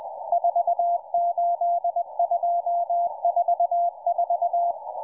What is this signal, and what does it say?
Morse code, heard as a keyed tone, about 20 wpm, 690 Hz, with radio noise. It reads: 48244